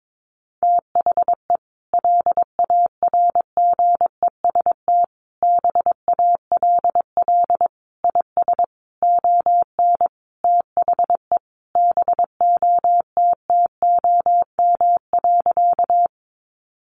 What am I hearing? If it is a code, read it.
Morse code, heard as a keyed tone, about 22 wpm, 700 Hz. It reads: THE LARGEST BALL IS ON THE BOTTOM.
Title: Easy Phrases 22wpm